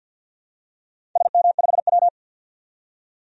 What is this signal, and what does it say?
Morse code, heard as a keyed tone, about 51 wpm, 700 Hz. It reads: SM5W
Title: DXCC-100 Call Signs - 4 Characters - 50wpm